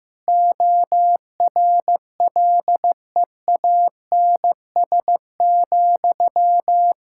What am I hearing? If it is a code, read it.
Morse code, heard as a keyed tone, about 15 wpm, 700 Hz. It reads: ORLEANS,
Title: K1USN SST QSO Practice 15wpm